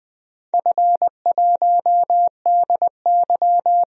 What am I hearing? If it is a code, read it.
Morse code, heard as a keyed tone, about 20 wpm, 700 Hz. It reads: F1DY
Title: DXCC-100 Call Signs - 4 Characters - 20wpm